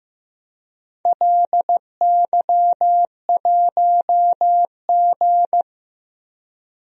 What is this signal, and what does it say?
Morse code, heard as a keyed tone, about 15 wpm, 700 Hz. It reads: LY1G
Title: DXCC-100 Call Signs - 4 Characters - Repeated 3x - 15wpm